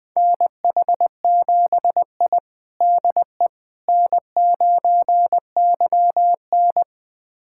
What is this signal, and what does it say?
Morse code, heard as a keyed tone, about 21 wpm, 700 Hz. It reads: NH7I DE N9YN
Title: USA Callsign DE Callsign - 4 Characters - 20wpm